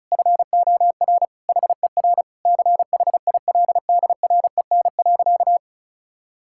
Morse code, heard as a keyed tone, about 34 wpm, 700 Hz. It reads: FOR HER CHILDREN.
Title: Sentences from Top 1000 Words - Review Part 6 - 35wpm